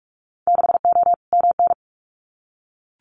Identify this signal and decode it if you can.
Morse code, heard as a keyed tone, about 44 wpm, 700 Hz. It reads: 6O MN
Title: Winter Field Day Exchanges Encore 45wpm